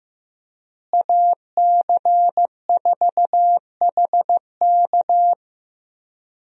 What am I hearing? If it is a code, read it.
Morse code, heard as a keyed tone, about 15 wpm, 700 Hz. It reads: AC4HK
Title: USA Call Signs - 5 Characters - Repeated 2x - 15wpm